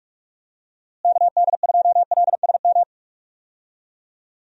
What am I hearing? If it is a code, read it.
Morse code, heard as a keyed tone, about 46 wpm, 700 Hz. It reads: KD2LSM
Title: USA Call Signs - 6 Characters - Repeated 3x - 45wpm